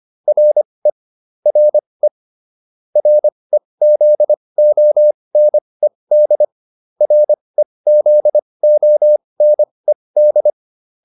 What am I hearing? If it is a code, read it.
Morse code, heard as a keyed tone, about 25 wpm, spaced as for 20 wpm, 600 Hz. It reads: RE RE REZONED REZONED
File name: RE7 20